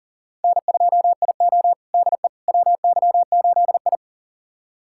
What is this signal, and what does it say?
Morse code, heard as a keyed tone, about 38 wpm, 700 Hz. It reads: N2IO DE WY8I
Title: USA Callsign DE Callsign - 4 Characters - 40wpm